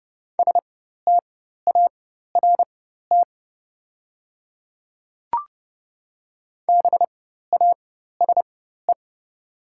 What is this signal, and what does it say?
Morse code, heard as a keyed tone, about 30 wpm, spaced as for 14 wpm, 700 Hz. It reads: START BASE
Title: 4x Character Spacing - Top 300 Words - 30wpm